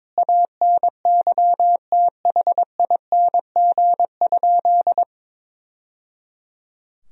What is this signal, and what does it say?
Morse code, heard as a keyed tone, about 22 wpm, 700 Hz. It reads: ANYTHING?
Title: Sentences from Top 1000 Words - Review Part 2 - 22wpm